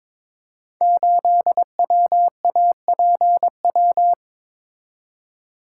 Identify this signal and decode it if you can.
Morse code, heard as a keyed tone, about 22 wpm, 700 Hz. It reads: 8WAPW
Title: Code Groups - 5 Letter:Numbers - 22wpm